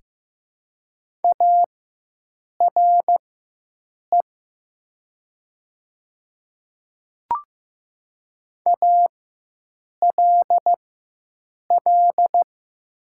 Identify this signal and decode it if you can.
Morse code, heard as a keyed tone, about 15 wpm, spaced as for 7 wpm, 700 Hz. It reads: ARE ALL
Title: 4x Character Spacing - Top 100 Words - 15wpm